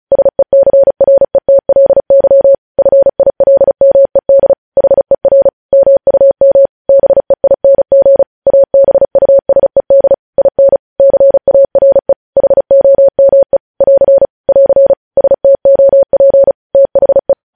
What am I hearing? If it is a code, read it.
Morse code, heard as a keyed tone, about 36 wpm, 550 Hz. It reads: SECRETLY FILMED HER MUM BEING ABUSED IN CARE HOME <AR> <AR> STOP THE
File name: podcast / News headlines-1x-35x35